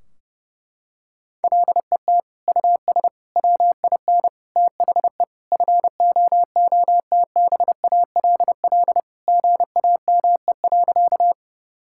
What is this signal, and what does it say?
Morse code, heard as a keyed tone, about 30 wpm, 700 Hz. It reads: LET US WIN THE FOOTBALL GAME.
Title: Sentences from Top 1600 Words 30wpm